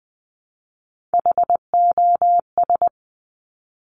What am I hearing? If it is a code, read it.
Morse code, heard as a keyed tone, about 20 wpm, 700 Hz. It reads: HOS
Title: Top Three Letter Combinations in Words 20wpm